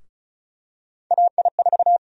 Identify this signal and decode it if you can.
Morse code, heard as a keyed tone, about 34 wpm, 700 Hz. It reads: AI4
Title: USA Call Sign Prefixes - 35wpm